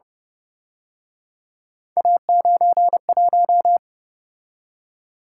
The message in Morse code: A91

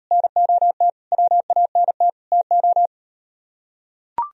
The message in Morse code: NOT WANT TO

